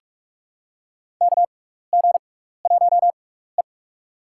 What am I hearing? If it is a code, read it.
Morse code, heard as a keyed tone, about 47 wpm, spaced as for 16 wpm, 700 Hz. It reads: KG1E
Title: USA Call Signs - 6x Character Spacing - 45wpm